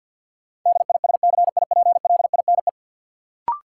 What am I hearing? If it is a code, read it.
Morse code, heard as a keyed tone, about 51 wpm, 700 Hz. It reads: DISCIPLINE
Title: Sentences from Top 900 Words - New Word Drill - 50wpm